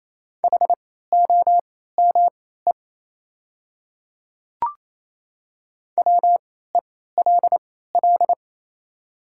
Morse code, heard as a keyed tone, about 28 wpm, spaced as for 16 wpm, 700 Hz. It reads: HOME WELL
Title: 3x Character Spacing - Top 300 Words - 28wpm